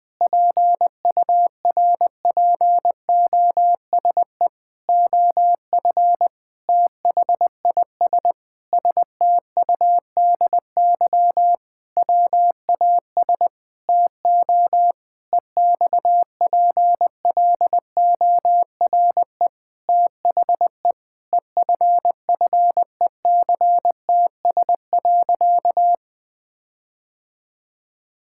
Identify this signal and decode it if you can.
Morse code, heard as a keyed tone, about 20 wpm, 700 Hz. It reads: PURPOSE OF THIS STUDY WAS TO EXPLORE THE EFFECTS.
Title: Sentences with 5-Word Phrases - 20wpm